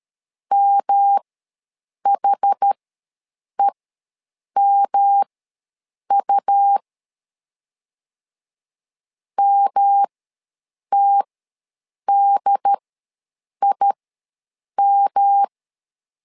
Morse code, heard as a keyed tone, about 13 wpm, spaced as for 7 wpm, 800 Hz. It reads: MHEMU MTDIM
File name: LEZIONI / 3 MWRO / 3 M ws 12wpm 1745